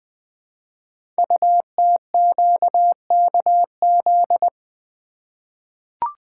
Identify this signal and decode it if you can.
Morse code, heard as a keyed tone, about 20 wpm, 700 Hz. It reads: UTQKZ
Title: Code Groups - 5 Letters - 20wpm